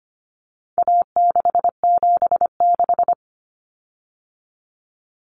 Morse code, heard as a keyed tone, about 25 wpm, 700 Hz. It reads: A676
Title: Common Cut Numbers - 4-Digit 25wpm